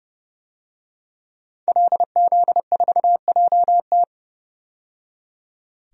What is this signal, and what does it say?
Morse code, heard as a keyed tone, about 32 wpm, 700 Hz. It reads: LZ4JT
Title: DXCC-100 Call Signs - 5 Characters - 30wpm